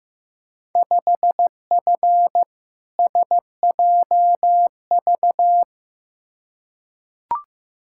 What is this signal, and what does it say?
Morse code, heard as a keyed tone, about 16 wpm, 700 Hz. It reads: 5F SJV
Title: Field Day Exchanges Encore - 15wpm